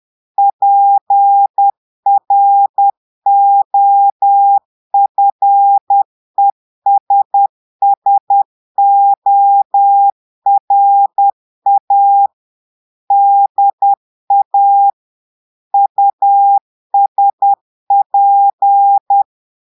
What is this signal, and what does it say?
Morse code, heard as a keyed tone, about 10 wpm, 800 Hz. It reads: PROFESSORA DA USP
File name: Portugais - texte en clair 1 - vitesse - 10 mots minute